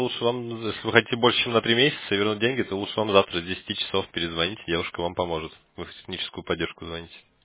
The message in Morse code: none

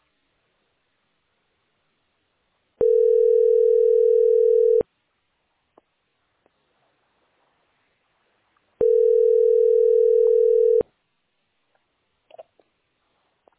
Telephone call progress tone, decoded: ringback tone

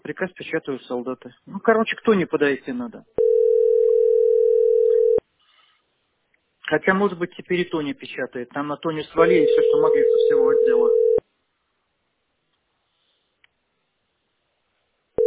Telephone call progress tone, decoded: ringback tone

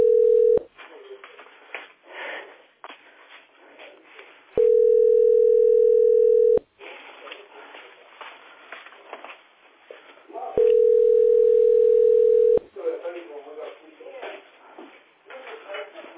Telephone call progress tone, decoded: ringback tone

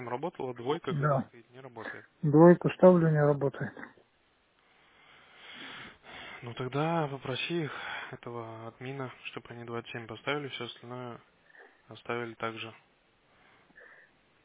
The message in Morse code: none

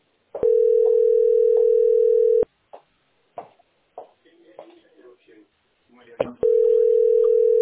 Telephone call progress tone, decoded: ringback tone

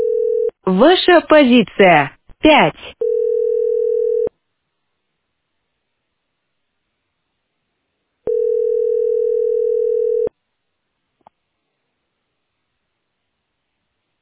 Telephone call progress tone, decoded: ringback tone

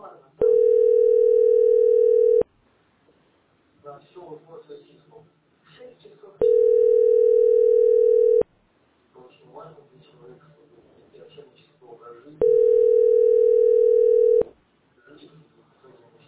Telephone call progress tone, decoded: ringback tone

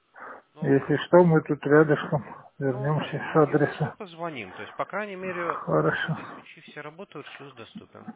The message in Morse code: none

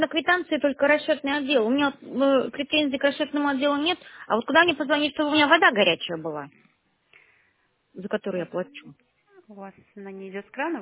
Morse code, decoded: none

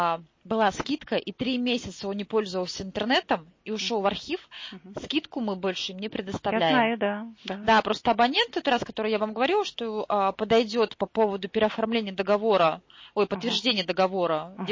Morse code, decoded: none

